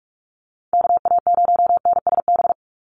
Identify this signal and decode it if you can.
Morse code, heard as a keyed tone, about 46 wpm, 700 Hz. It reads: KA0NSB
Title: USA Call Signs 45wpm